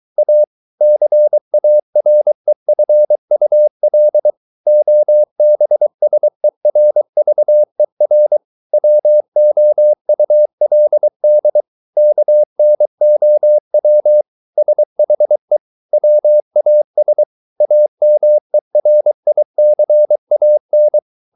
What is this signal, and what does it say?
Morse code, heard as a keyed tone, about 23 wpm, 600 Hz. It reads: A CAREFUL OBSERVER WOULD KNOW SHE WAS AMERICAN